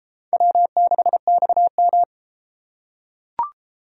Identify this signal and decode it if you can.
Morse code, heard as a keyed tone, about 34 wpm, 700 Hz. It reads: W6XM